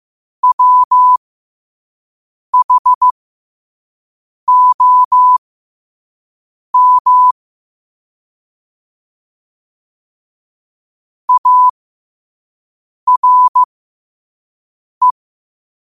Morse code, heard as a keyed tone, about 15 wpm, spaced as for 5 wpm, 1000 Hz. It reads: WHOM ARE
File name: qotdfiles / 5wpm